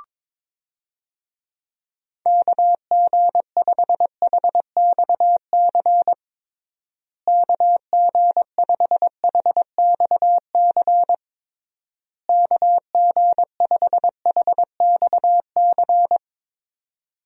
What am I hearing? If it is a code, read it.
Morse code, heard as a keyed tone, about 22 wpm, 700 Hz. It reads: KG5HXC KG5HXC KG5HXC
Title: USA Call Signs - 6 Characters - Repeated 3x - 22wpm